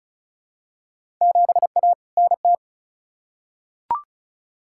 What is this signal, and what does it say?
Morse code, heard as a keyed tone, about 36 wpm, 700 Hz. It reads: 7A NT
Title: Field Day Exchanges Encore - 35wpm